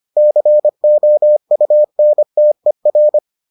Morse code, heard as a keyed tone, about 25 wpm, 600 Hz. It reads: COUNTER